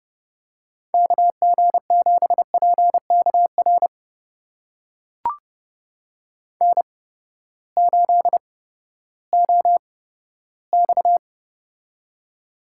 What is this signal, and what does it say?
Morse code, heard as a keyed tone, about 30 wpm, spaced as for 15 wpm, 700 Hz. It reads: KG7PKR N8OX
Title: USA Call Signs - 8x Character Spacing - 30wpm